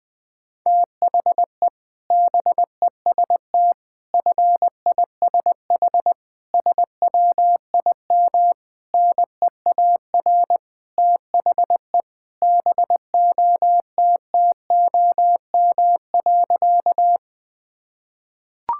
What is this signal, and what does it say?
Morse code, heard as a keyed tone, about 20 wpm, 700 Hz. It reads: THE BEST FISH SWIM NEAR THE BOTTOM.
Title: Sentences from Top 1000 Words - Review Part 6 - 20wpm